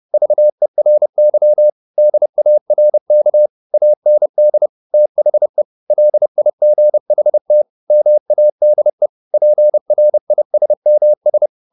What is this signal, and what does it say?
Morse code, heard as a keyed tone, about 30 wpm, 600 Hz. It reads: VERY DARK AND THE LIGHT MADE PRISMS
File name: the old man and the sea / the old man and the sea 1 30wpm